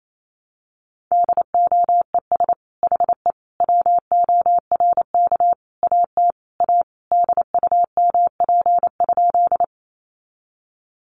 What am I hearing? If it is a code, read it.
Morse code, heard as a keyed tone, about 28 wpm, 700 Hz. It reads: DOES HE WORK AT A DUMP?
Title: Sentences from Top 1300 Words 28wpm